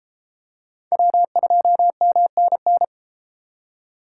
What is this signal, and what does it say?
Morse code, heard as a keyed tone, about 31 wpm, 700 Hz. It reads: W2MNN